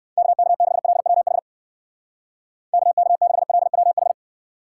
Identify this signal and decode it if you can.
Morse code, heard as a keyed tone, about 57 wpm, 700 Hz. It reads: KK6LWH KK6LWH